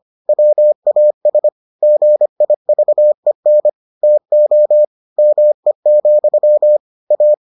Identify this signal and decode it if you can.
Morse code, heard as a keyed tone, about 25 wpm, 600 Hz. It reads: WAS GIVEN TO ME, A